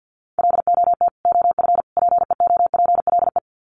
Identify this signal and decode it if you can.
Morse code, heard as a keyed tone, about 50 wpm, 700 Hz. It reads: LOT OF PEOPLE
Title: Sets of 3 Words 50wpm